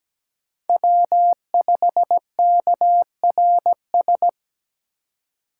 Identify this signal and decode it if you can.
Morse code, heard as a keyed tone, about 17 wpm, 700 Hz. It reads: W5KRS
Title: USA Call Signs - 5 Characters - 4x Character Spacing - 17wpm